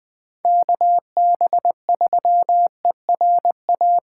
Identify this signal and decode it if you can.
Morse code, heard as a keyed tone, about 20 wpm, 700 Hz. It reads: KB3ERA
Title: USA Call Signs - 6 Characters - 4x Character Spacing - 20wpm